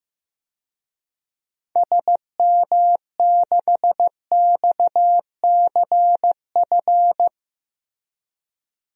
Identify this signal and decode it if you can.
Morse code, heard as a keyed tone, about 15 wpm, 700 Hz. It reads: SM6XCF
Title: DXCC-100 Call Signs - 6 Characters - Repeated 3x - 15wpm